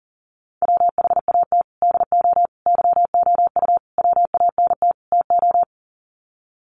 Morse code, heard as a keyed tone, about 40 wpm, 700 Hz. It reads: WHAT DO YOU WANT TO
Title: Sets of 5 Words 40wpm